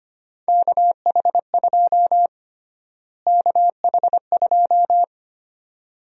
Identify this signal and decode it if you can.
Morse code, heard as a keyed tone, about 25 wpm, 700 Hz. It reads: KH2 KH2
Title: USA Call Sign Prefixes - Repeated 2x - 25wpm